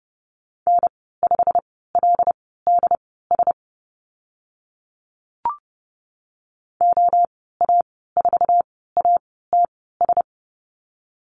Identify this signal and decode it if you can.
Morse code, heard as a keyed tone, about 30 wpm, spaced as for 17 wpm, 700 Hz. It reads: N5LDS OA4ATS
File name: Lesson 15g - Number 5 - Call Signs 30wpm